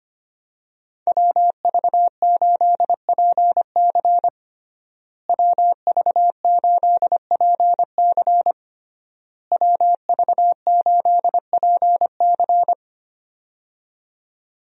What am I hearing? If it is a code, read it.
Morse code, heard as a keyed tone, about 25 wpm, 700 Hz. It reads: WV8PC WV8PC WV8PC
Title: USA Call Signs - 5 Characters - Repeated 3x - 25wpm